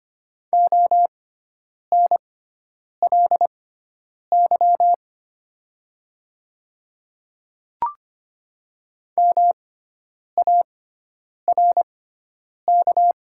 Morse code, heard as a keyed tone, about 25 wpm, spaced as for 9 wpm, 700 Hz. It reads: ONLY MARK
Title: 6x Character Spacing - Top 300 Words - 25wpm